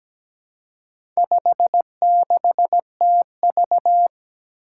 Morse code, heard as a keyed tone, about 17 wpm, 700 Hz. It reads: 56TV
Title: Four Challenging Letter-Number Combinations 17wpm